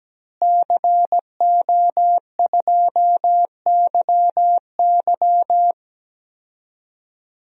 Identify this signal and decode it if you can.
Morse code, heard as a keyed tone, about 17 wpm, 700 Hz. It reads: CO2YY
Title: DXCC-100 Call Signs - 5 Characters - Repeated 3x - 17wpm